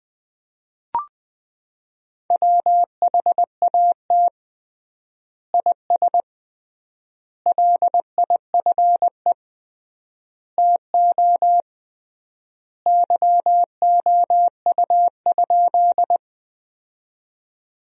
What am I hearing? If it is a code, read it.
Morse code, heard as a keyed tone, about 21 wpm, 700 Hz. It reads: WHAT IS LIFE TO YOU?
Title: 3x Word-Spacing - Sentences from Top 300 Words - 20wpm